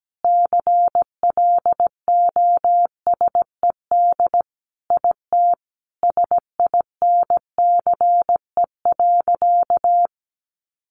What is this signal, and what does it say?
Morse code, heard as a keyed tone, about 17 wpm, 700 Hz. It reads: CLOSED IT SINCE.
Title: Sentences from Top 1400 Words 17wpm